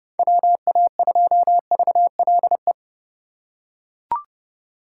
Morse code, heard as a keyed tone, about 29 wpm, 700 Hz. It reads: WA2VLE